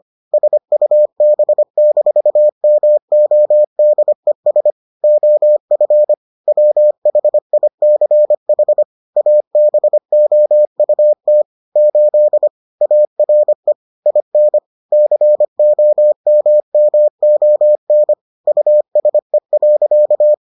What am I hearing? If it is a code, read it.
Morse code, heard as a keyed tone, about 25 wpm, 600 Hz. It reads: SUB-MODES OF WHICH ABOUT 8 ARE IN COMMON USE.